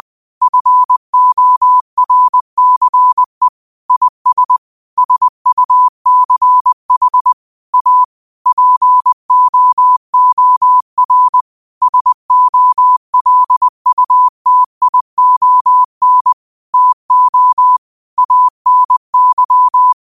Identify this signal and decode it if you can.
Morse code, heard as a keyed tone, about 20 wpm, 1000 Hz. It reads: FORCE IS SUCH A POOR SOLUTION TO ANY
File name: qotdfiles / 20wpm